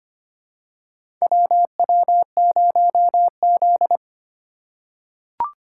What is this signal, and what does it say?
Morse code, heard as a keyed tone, about 25 wpm, 700 Hz. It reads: WW0Z
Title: USA Call Signs - 2x Character Spacing - 25wpm